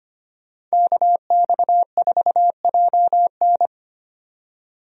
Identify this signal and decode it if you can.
Morse code, heard as a keyed tone, about 25 wpm, 700 Hz. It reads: KX4JN
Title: USA Call Signs - 5 Characters - 25wpm